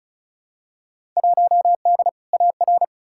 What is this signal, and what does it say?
Morse code, heard as a keyed tone, about 36 wpm, 700 Hz. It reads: 1D AR
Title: Field Day Exchanges - 35wpm